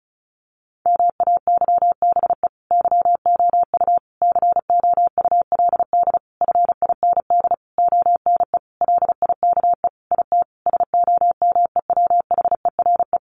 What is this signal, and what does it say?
Morse code, heard as a keyed tone, about 34 wpm, 700 Hz. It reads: MAYBE YOU COULD FIND ONE LIKE IT SOMEWHERE
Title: Sentences from Top 1400 Words 35wpm